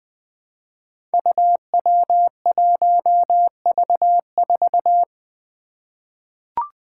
Morse code, heard as a keyed tone, about 20 wpm, 700 Hz. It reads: UW1V4
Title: Five Challenging Letter-Number Combinations 20wpm